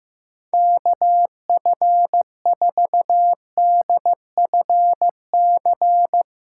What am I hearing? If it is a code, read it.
Morse code, heard as a keyed tone, about 15 wpm, 700 Hz. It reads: KF4DFC
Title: SOTA QSO Practice 15wpm